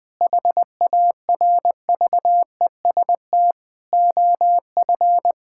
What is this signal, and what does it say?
Morse code, heard as a keyed tone, about 19 wpm, 700 Hz. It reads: HARVEST OF
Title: Difficult Sentences with Hints - 20wpm